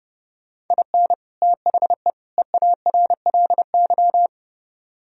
Morse code, heard as a keyed tone, about 29 wpm, 700 Hz. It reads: IN THE EARLY
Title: Speed-Racing - Sets of 3 Words 30wpm